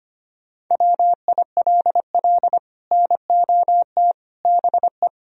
Morse code, heard as a keyed tone, about 33 wpm, 700 Hz. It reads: WILL NOT BE